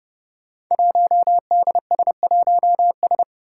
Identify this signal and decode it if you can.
Morse code, heard as a keyed tone, about 30 wpm, 700 Hz. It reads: 1DS1S
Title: Five Challenging Letter-Number Combinations 30wpm